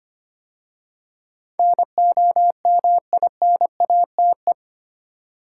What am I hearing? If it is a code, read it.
Morse code, heard as a keyed tone, about 25 wpm, 700 Hz. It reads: NOMINATE
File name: 8 Letter Words - First 3 Letters Given - 25wpm